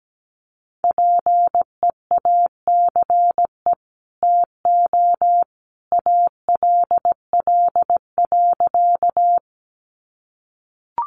Morse code, heard as a keyed tone, about 17 wpm, 700 Hz. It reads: PEACE TO ALL.
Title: Sentences from Top 1500 Words 17wpm